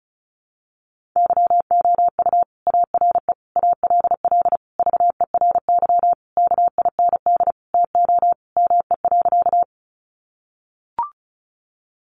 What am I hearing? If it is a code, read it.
Morse code, heard as a keyed tone, about 34 wpm, 700 Hz. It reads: YOU ARE ALL VERY KIND TO ME.
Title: Sentences from Top 200 Words 35wpm